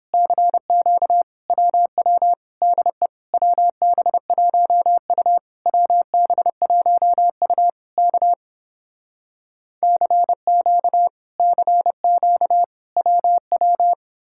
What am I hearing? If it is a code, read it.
Morse code, heard as a keyed tone, about 30 wpm, 700 Hz. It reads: CQ WW DE WB1U WB1U K CQ CQ WW